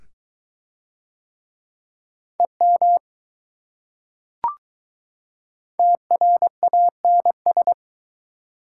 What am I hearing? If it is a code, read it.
Morse code, heard as a keyed tone, about 20 wpm, 700 Hz. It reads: EM TRANS